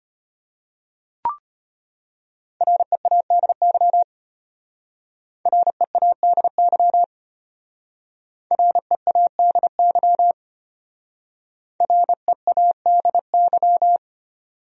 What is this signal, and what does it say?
Morse code, heard as a keyed tone, about 32 wpm, 700 Hz. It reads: READY READY READY READY